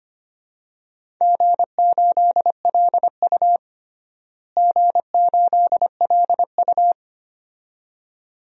Morse code, heard as a keyed tone, about 25 wpm, 700 Hz. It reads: G8LU G8LU
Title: DXCC-100 Call Signs - 4 Characters - Repeated 2x - 25wpm